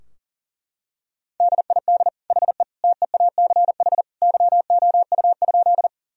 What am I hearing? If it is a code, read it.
Morse code, heard as a keyed tone, about 37 wpm, 700 Hz. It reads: DID HE TEACH YOU?
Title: Sentences from Top 1000 Words - Review Part 2 - 40wpm